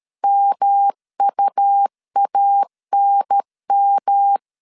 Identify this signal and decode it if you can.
Morse code, heard as a keyed tone, about 13 wpm, 800 Hz. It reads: MUANM
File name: CW / DET / 1 3 1500C S B 12wpm 2828